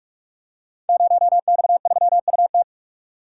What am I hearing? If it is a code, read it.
Morse code, heard as a keyed tone, about 44 wpm, 700 Hz. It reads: 0X3UT
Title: Code Groups - 5 Letter:Numbers - 45wpm